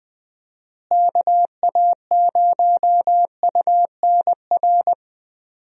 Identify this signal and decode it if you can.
Morse code, heard as a keyed tone, about 20 wpm, 700 Hz. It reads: KA0UNR
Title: USA Call Signs - 6 Characters - Repeated 2x - 20wpm